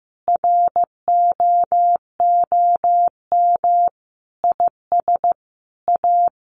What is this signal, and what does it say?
Morse code, heard as a keyed tone, about 15 wpm, 700 Hz. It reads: ROOM IS A
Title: Additional Easy Phrases 15wpm